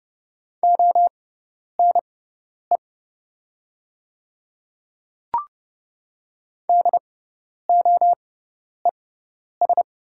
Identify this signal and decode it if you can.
Morse code, heard as a keyed tone, about 30 wpm, spaced as for 10 wpm, 700 Hz. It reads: ONE DOES
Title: 6x Character Spacing - Top 300 Words - 30wpm